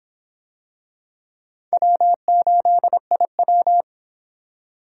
Morse code, heard as a keyed tone, about 25 wpm, 700 Hz. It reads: W8IW